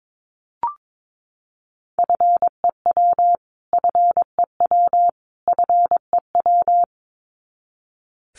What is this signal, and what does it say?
Morse code, heard as a keyed tone, about 22 wpm, 700 Hz. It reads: FEW FEW FEW
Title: Top 500 Words Repeated 22wpm